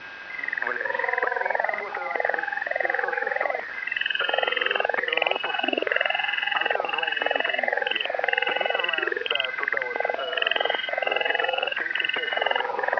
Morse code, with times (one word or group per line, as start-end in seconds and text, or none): none